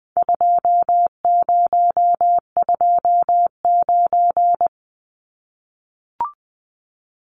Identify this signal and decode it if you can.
Morse code, heard as a keyed tone, about 20 wpm, 700 Hz. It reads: 2029
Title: Numbers - 4-Digit 20wpm